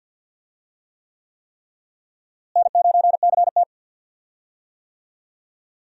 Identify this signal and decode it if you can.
Morse code, heard as a keyed tone, about 46 wpm, 700 Hz. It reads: N9CT
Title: USA Call Signs - 4 Characters - Repeated 3x - 50wpm